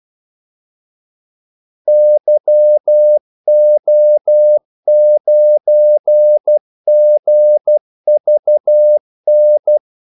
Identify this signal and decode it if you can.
Morse code, heard as a keyed tone, about 12 wpm, 600 Hz. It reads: YO9GVN
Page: 2 ekstra ordmellemrum mellem kaldesignalerne